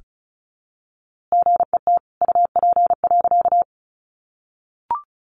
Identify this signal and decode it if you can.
Morse code, heard as a keyed tone, about 34 wpm, 700 Hz. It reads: GET UP.
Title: A Sound Of Thunder 35wpm